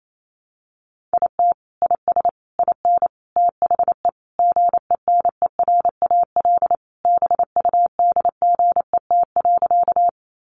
Morse code, heard as a keyed tone, about 28 wpm, 700 Hz. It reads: IT IS IN THE GENERAL BUDGET.
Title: Sentences from Top 1200 Words 28wpm